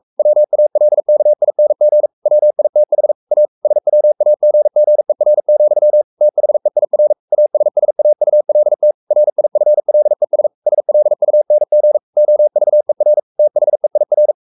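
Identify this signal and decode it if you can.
Morse code, heard as a keyed tone, about 44 wpm, 600 Hz. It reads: WALKING WITH A SWAGGER, THEIR ASSAULT RIFLES SLUNG OVER THEIR
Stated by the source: Morse code at 43 wpm sending Ken Follett's novel Never in English, from CW med Gnister.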